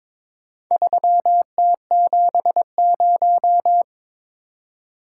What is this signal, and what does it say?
Morse code, heard as a keyed tone, about 22 wpm, 700 Hz. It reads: 3T70
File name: practice / 22wpm / Four Letter-Number Combinations - 22wpm